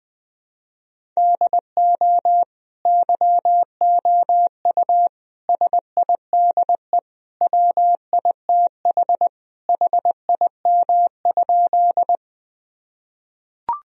Morse code, heard as a keyed tone, about 20 wpm, 700 Hz. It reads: DO YOU SIDE WITH HIM?
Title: Sentences from Top 1000 Words - Review Part 1 - 20wpm